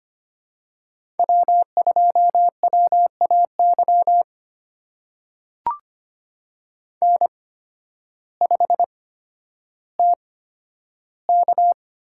Morse code, heard as a keyed tone, about 25 wpm, spaced as for 12 wpm, 700 Hz. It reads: W2WAY N5TK
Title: USA Call Signs - 8x Character Spacing - 25wpm